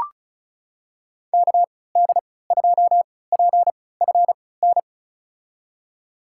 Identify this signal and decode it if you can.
Morse code, heard as a keyed tone, about 36 wpm, spaced as for 20 wpm, 700 Hz. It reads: KD2PFN
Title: USA Call Signs - 3x Character Spacing - 35wpm